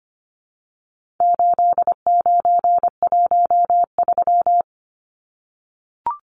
Morse code, heard as a keyed tone, about 25 wpm, 700 Hz. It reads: 8913
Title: Numbers - 4-Digit 25wpm